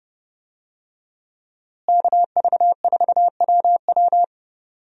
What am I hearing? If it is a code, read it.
Morse code, heard as a keyed tone, about 31 wpm, 700 Hz. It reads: KV4WW
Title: USA Call Signs - 5 Characters - Repeated 2x - 30wpm